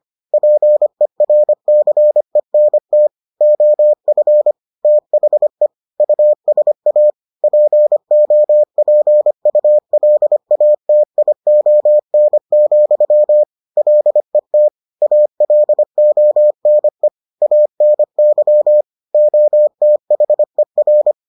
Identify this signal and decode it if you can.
Morse code, heard as a keyed tone, about 25 wpm, 600 Hz. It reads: PERCENT OF THE USA POPULATION, LET ALONE ANY OTHER